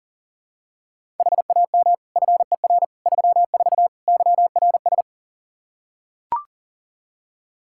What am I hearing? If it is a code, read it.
Morse code, heard as a keyed tone, about 40 wpm, 700 Hz. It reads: HAM FER 34 YRS
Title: Practice QSO - Exchange 3 - More Info - 40wpm